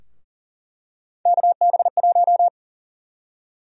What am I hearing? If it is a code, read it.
Morse code, heard as a keyed tone, about 40 wpm, 700 Hz. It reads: KB1
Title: USA Call Sign Prefixes - Repeated 2x - 40wpm